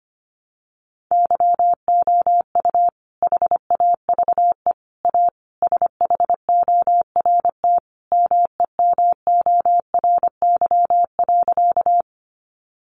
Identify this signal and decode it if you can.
Morse code, heard as a keyed tone, about 25 wpm, 700 Hz. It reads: YOU HAVE A SHORT MEMORY.
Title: Sentences from Top 1300 Words 25wpm